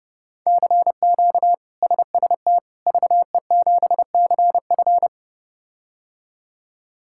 Morse code, heard as a keyed tone, about 32 wpm, 700 Hz. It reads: CQ SST VE7CF
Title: K1USN SST QSO Practice - 30wpm